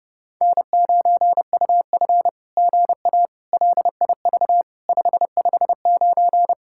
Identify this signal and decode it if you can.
Morse code, heard as a keyed tone, about 30 wpm, 700 Hz. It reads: N9UF GA LIV 559
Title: SOTA QSO Practice 30wpm